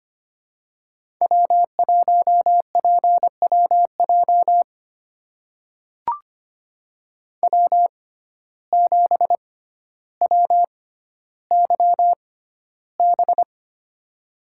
Morse code, heard as a keyed tone, about 25 wpm, spaced as for 13 wpm, 700 Hz. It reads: W1PWJ W7WYB
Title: USA Call Signs - 5 Characters - 6x Character Spacing - 25wpm